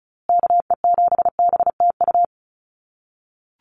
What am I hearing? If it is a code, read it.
Morse code, heard as a keyed tone, about 34 wpm, 700 Hz. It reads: KE7BTU
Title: USA Call Signs Repeated 2x 35wpm